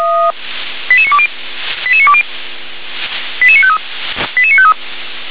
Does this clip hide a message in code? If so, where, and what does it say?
Yes: five-tone selective call (ZVEI) from 0.9 to 4.7 s, heard as quick runs of five tones, tones of about 70 ms. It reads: 88920 88920 88953 88953